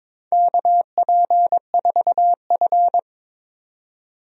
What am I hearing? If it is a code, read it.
Morse code, heard as a keyed tone, about 22 wpm, 700 Hz. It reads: KP4F
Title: DXCC-100 Call Signs - 4 Characters - 22wpm